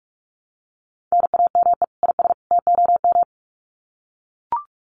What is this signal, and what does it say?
Morse code, heard as a keyed tone, about 44 wpm, 700 Hz. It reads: NAME IS TOM
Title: Common QSO Phrases 45wpm